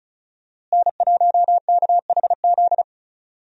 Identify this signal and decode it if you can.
Morse code, heard as a keyed tone, about 34 wpm, 700 Hz. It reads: N1KHZ
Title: USA Call Signs - 5 Characters - Repeated 2x - 35wpm